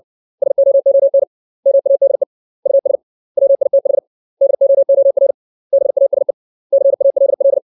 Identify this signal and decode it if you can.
Morse code, heard as a keyed tone, about 60 wpm, 550 Hz. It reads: SOON MADE US WITH DOOR BASE CALL
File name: 632 words / 632 Words 60WPM